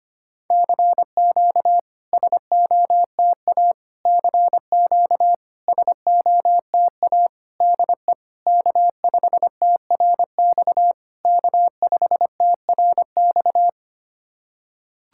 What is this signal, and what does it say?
Morse code, heard as a keyed tone, about 25 wpm, 690 Hz. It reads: CQ SOTA CQ SOTA DE K5TRX K5TRX
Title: SOTA QSO Practice 25wpm